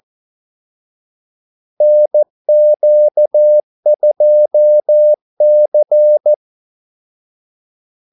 Morse code, heard as a keyed tone, about 14 wpm, 600 Hz. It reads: NQ2C
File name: Callsigns-14wpm